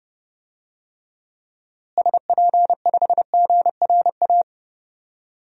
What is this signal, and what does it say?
Morse code, heard as a keyed tone, about 30 wpm, 700 Hz. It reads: SP5GRA
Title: DXCC-100 Call Signs - 6 Characters - Repeated 2x - 30wpm